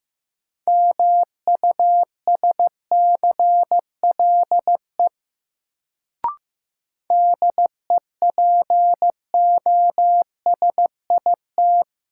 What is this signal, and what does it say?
Morse code, heard as a keyed tone, about 15 wpm, 700 Hz. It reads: MUSCLE DEPOSIT
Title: Sentences from Top 1300 Words - New Word Drill - 15wpm